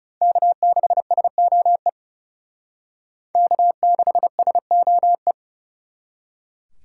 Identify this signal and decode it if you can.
Morse code, heard as a keyed tone, about 38 wpm, 700 Hz. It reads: K6SOE K6SOE